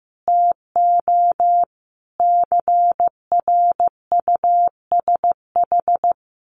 Morse code, heard as a keyed tone, about 15 wpm, 700 Hz. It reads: TO CRUSH
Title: A Sound Of Thunder 15wpm